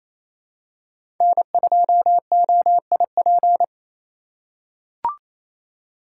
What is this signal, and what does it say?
Morse code, heard as a keyed tone, about 28 wpm, 700 Hz. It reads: N2OIP
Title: USA Call Signs - 3x Character Spacing - 28wpm